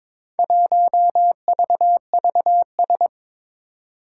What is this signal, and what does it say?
Morse code, heard as a keyed tone, about 22 wpm, 700 Hz. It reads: 1VVS
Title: Four Challenging Letter-Number Combinations 22wpm